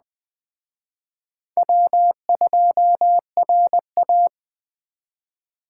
Morse code, heard as a keyed tone, about 20 wpm, 700 Hz. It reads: W2RA
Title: Speed Racing - USA 1x2 Call Signs 17wpm